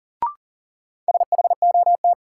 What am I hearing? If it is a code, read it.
Morse code, heard as a keyed tone, about 40 wpm, 700 Hz. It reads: SHOT
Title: Sentences from Top 1100 Words - New Word Drill Alt - 40wpm